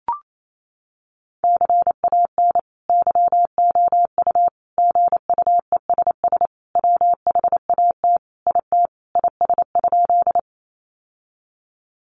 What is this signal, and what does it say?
Morse code, heard as a keyed tone, about 28 wpm, 700 Hz. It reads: CAN YOU GUESS WHAT IT IS?
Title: Sentences from Top 1500 Words 28wpm